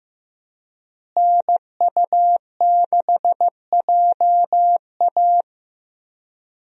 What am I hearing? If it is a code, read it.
Morse code, heard as a keyed tone, about 15 wpm, 700 Hz. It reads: NU6JA
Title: USA Call Signs - 5 Characters - Repeated 2x - 15wpm